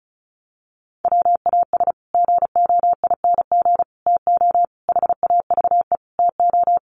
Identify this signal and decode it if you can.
Morse code, heard as a keyed tone, about 34 wpm, 700 Hz. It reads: WAS GOING TO HAVE TO
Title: Sets of 5 Words Encore 35wpm